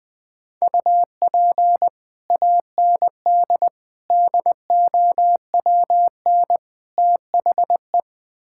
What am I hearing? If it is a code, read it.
Morse code, heard as a keyed tone, about 20 wpm, 700 Hz. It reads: UP AND DOWN THE
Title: Build A Phrase - 4-Word Phrases - 20wpm